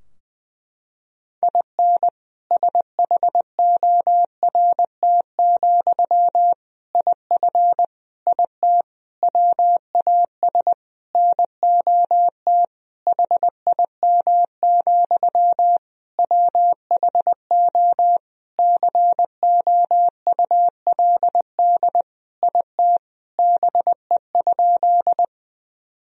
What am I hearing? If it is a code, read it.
Morse code, heard as a keyed tone, about 20 wpm, 700 Hz. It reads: IN SHORT, IF IT WAS NOT HIM, WHO COULD IT BE?
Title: Sentences from Top 1000 Words - Review Part 2 - 20wpm